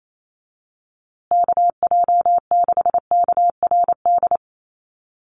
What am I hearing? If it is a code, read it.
Morse code, heard as a keyed tone, about 28 wpm, 700 Hz. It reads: KJ6KRD
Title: USA Call Signs Repeated 2x 28wpm